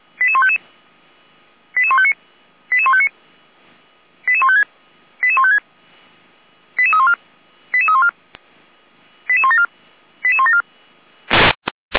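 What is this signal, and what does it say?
Five-tone selective call (ZVEI), heard as quick runs of five tones, tones of about 70 ms. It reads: 88155 88159 88159 88156 88156 88314 88314 88174 88174
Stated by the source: Received by SDR; ZVEI five-tone call 88171, channel 455.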